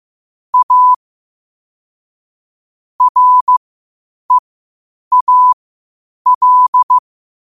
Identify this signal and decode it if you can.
Morse code, heard as a keyed tone, about 15 wpm, spaced as for 8 wpm, 1000 Hz. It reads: A REAL
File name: qotdfiles / 8wpm